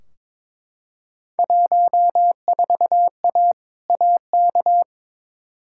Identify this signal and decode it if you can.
Morse code, heard as a keyed tone, about 22 wpm, 700 Hz. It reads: 14A AK
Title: Field Day Exchanges Encore - 22wpm